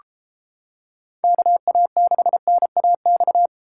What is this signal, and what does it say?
Morse code, heard as a keyed tone, about 34 wpm, 700 Hz. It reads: KA6NAX